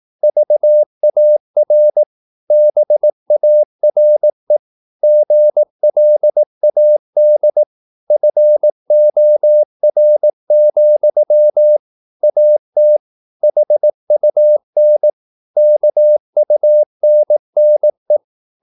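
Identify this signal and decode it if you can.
Morse code, heard as a keyed tone, about 18 wpm, 600 Hz. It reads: VAR BARE GLAD FOR, AT HUN KUNNE
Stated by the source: Slow (QRS) Morse practice at 18 wpm, CW med Gnister.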